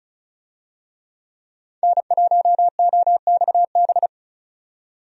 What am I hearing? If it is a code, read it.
Morse code, heard as a keyed tone, about 34 wpm, 700 Hz. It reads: N1OXB
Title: USA Call Signs - 5 Characters - 35wpm